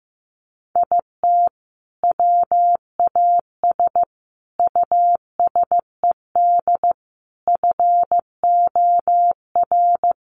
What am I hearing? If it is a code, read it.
Morse code, heard as a keyed tone, about 15 wpm, 700 Hz. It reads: IT WAS USED FOR
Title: Sentences from Top 1400 Words 15wpm